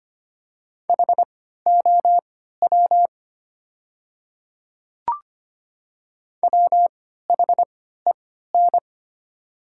Morse code, heard as a keyed tone, about 25 wpm, spaced as for 14 wpm, 700 Hz. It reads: HOW WHEN
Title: 3x Character Spacing - Top 100 Words - 25wpm